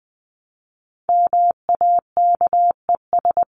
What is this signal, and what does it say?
Morse code, heard as a keyed tone, about 20 wpm, 700 Hz. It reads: MAKES
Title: ES:S Suffix - Top 100 Words with Base Words 20wpm